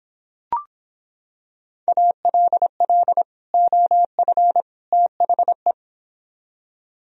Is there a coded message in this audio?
Yes: Morse code from 1.9 to 5.7 s, heard as a keyed tone, about 26 wpm, 700 Hz. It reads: ALL OF THE